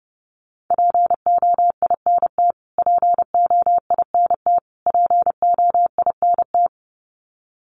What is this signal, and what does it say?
Morse code, heard as a keyed tone, about 30 wpm, 700 Hz. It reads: POINT POINT POINT
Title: Top 300 Words Repeated 30wpm